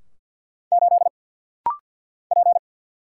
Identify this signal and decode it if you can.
Morse code, heard as a keyed tone, about 50 wpm, 700 Hz. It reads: 8 P
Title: Single Letter-Number Alt - Rapid-Fire 50wpm